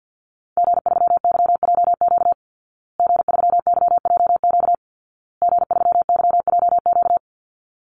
Morse code, heard as a keyed tone, about 48 wpm, 700 Hz. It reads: G3YJQ G3YJQ G3YJQ
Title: Call Signs Repeated 2x - 50wpm